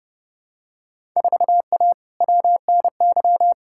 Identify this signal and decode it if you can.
Morse code, heard as a keyed tone, about 30 wpm, 700 Hz. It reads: 4A WNY
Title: Field Day Exchanges Encore - 30wpm